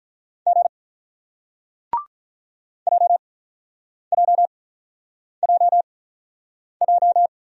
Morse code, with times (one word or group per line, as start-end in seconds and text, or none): 0.5-0.7 G
2.9-7.3 JJJJ